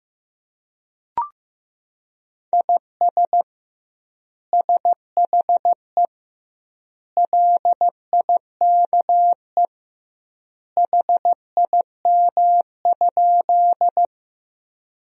Morse code, heard as a keyed tone, about 16 wpm, 700 Hz. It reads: IS SHE LIKE HIM?
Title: 2x Word-Spacing - Sentences from Top 100 Words - 15wpm